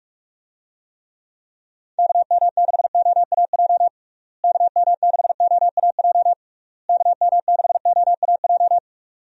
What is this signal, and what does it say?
Morse code, heard as a keyed tone, about 46 wpm, 700 Hz. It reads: KM6OAJ KM6OAJ KM6OAJ
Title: USA Call Signs - 6 Characters - Repeated 3x - 45wpm